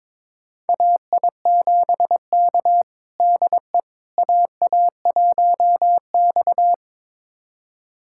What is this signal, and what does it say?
Morse code, heard as a keyed tone, about 22 wpm, 700 Hz. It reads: AI7K DE AA1X
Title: USA Callsign DE Callsign - 4 Characters - 22wpm